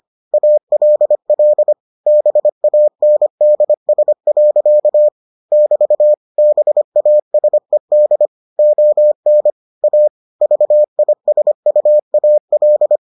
Morse code, heard as a keyed tone, about 25 wpm, 600 Hz. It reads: ALL BANDS. <BT> BASED ON A VISUAL